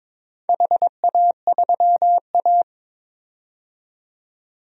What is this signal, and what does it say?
Morse code, heard as a keyed tone, about 22 wpm, 700 Hz. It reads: HA3A
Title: DXCC-100 Call Signs - 4 Characters - 22wpm